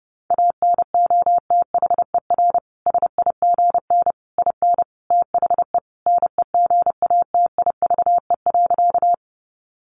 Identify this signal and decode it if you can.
Morse code, heard as a keyed tone, about 30 wpm, 700 Hz. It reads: ANOTHER SIGN IN THE NEGATIVE.
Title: Sentences from Top 1000 Words 30wpm